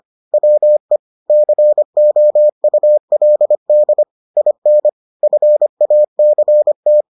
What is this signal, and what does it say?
Morse code, heard as a keyed tone, about 25 wpm, 600 Hz. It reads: WE COULD IN FACT